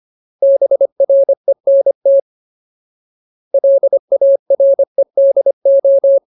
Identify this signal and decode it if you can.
Morse code, heard as a keyed tone, about 25 wpm, 550 Hz. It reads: BRENT LAREDO
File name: Ultimate Word List / 25WPM Ultimate